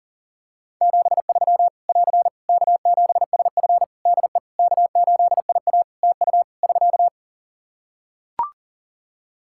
Morse code, heard as a keyed tone, about 40 wpm, 700 Hz. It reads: 73 <AR> K7SF DE K8IA TU <SK>
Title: Practice QSO - Exchange 4 - Ending - 40wpm